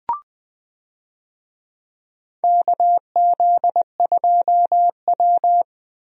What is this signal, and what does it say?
Morse code, heard as a keyed tone, about 20 wpm, 700 Hz. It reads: KZ2W
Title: USA Call Signs - 4 Characters - Repeated 3x - 20wpm